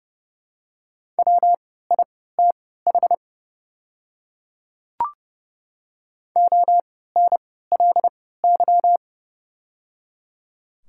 Morse code, heard as a keyed tone, about 30 wpm, spaced as for 17 wpm, 690 Hz. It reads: WITH ONLY
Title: 3x Character Spacing - Top 200 Words - 30wpm